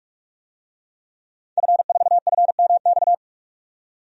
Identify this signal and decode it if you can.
Morse code, heard as a keyed tone, about 46 wpm, 700 Hz. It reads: F4FMX
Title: DXCC-100 Call Signs - 5 Characters - 45wpm